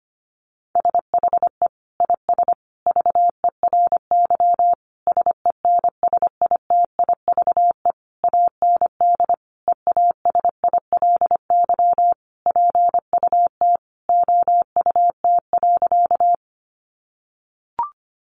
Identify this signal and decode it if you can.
Morse code, heard as a keyed tone, about 25 wpm, 700 Hz. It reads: SHE IS VERY SENSITIVE AND EASILY PUT OUT.
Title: Sentences from Top 1400 Words 25wpm